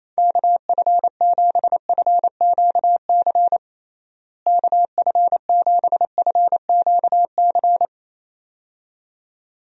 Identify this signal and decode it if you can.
Morse code, heard as a keyed tone, about 27 wpm, 700 Hz. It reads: KF7FQC KF7FQC
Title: USA Call Signs - 6 Characters - Repeated 2x - 28wpm